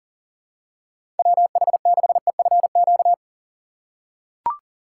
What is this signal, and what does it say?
Morse code, heard as a keyed tone, about 40 wpm, 700 Hz. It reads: WH6EFQ
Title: USA Call Signs - 6 Characters - 2x Character Spacing - 40wpm